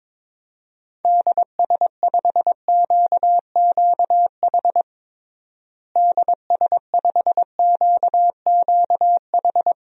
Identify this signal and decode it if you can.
Morse code, heard as a keyed tone, about 22 wpm, 700 Hz. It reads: DS5QQH DS5QQH
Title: DXCC-100 Call Signs - 6 Characters - Repeated 3x - 22wpm